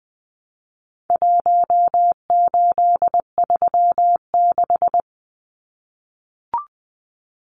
Morse code, heard as a keyed tone, about 20 wpm, 700 Hz. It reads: 1836
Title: Numbers - Contest Style - 4-Digit 20wpm